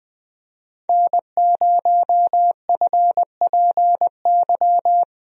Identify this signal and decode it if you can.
Morse code, heard as a keyed tone, about 20 wpm, 700 Hz. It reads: N0FPY
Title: USA Call Signs - 5 Characters - 20wpm